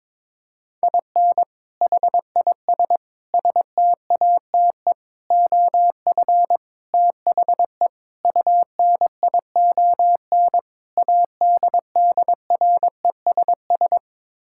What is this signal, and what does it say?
Morse code, heard as a keyed tone, about 22 wpm, 700 Hz. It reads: IN HIS STATE OF THE UNION ADDRESS
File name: practice / 22wpm / Sets of 7 Words - 22wpm